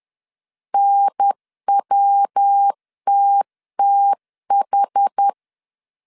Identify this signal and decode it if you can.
Morse code, heard as a keyed tone, about 10 wpm, 800 Hz. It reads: NWTTH
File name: 10wpm